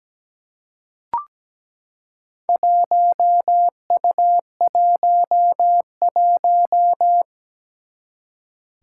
Morse code, heard as a keyed tone, about 17 wpm, 700 Hz. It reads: 1U11
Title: Four Challenging Letter-Number Combinations 17wpm